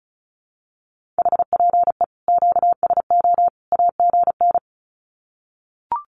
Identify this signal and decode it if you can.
Morse code, heard as a keyed tone, about 34 wpm, 700 Hz. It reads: HPE QSO AGN
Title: Common QSO Phrases 35wpm